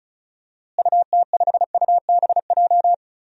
Morse code, heard as a keyed tone, about 36 wpm, 700 Hz. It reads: UT5UBJ
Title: DXCC-100 Call Signs - 6 Characters - Repeated 3x - 35wpm